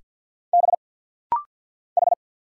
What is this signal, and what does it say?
Morse code, heard as a keyed tone, about 49 wpm, 700 Hz. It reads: B H
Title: Single Letter-Number Alt - Rapid-Fire 50wpm